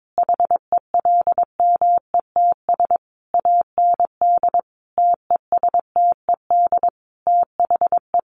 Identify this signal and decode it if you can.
Morse code, heard as a keyed tone, about 22 wpm, 700 Hz. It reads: HELMETS AND TESTED THE
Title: A Sound Of Thunder 22wpm